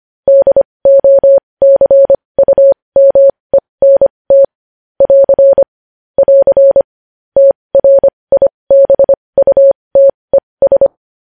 Morse code, heard as a keyed tone, about 25 wpm, spaced as for 20 wpm, 550 Hz. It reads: DOCUMENT <AR> <AR> TRIBUTES
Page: News Headlines in Morse Code at 20 WPM